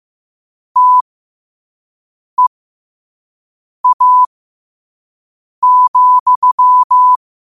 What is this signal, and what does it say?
Morse code, heard as a keyed tone, about 15 wpm, spaced as for 5 wpm, 1000 Hz. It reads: TEA,
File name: qotdfiles / 5wpm